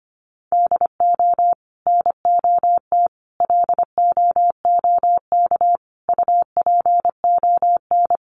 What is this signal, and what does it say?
Morse code, heard as a keyed tone, about 25 wpm, 700 Hz. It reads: DO NOT LOOK UPON
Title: Sentences from Top 1100 Words 25wpm